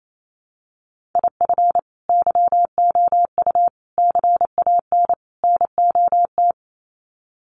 Morse code, heard as a keyed tone, about 28 wpm, 700 Hz. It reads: IF YOU CAN NOT
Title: Sets of 4 Words 28wpm